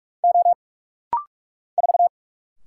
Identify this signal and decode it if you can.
Morse code, heard as a keyed tone, about 44 wpm, 690 Hz. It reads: O 4
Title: Single Letter-Number - Rapid-Fire 45wpm